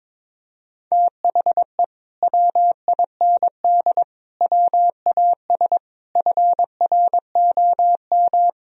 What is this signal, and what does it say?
Morse code, heard as a keyed tone, about 22 wpm, 700 Hz. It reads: THE WIND WAS FROM
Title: Sentences from Top 1000 Words - Review Part 2 - 22wpm